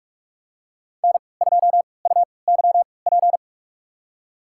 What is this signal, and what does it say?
Morse code, heard as a keyed tone, about 44 wpm, spaced as for 25 wpm, 700 Hz. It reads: N2UYP
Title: USA Call Signs - 3x Character Spacing - 45wpm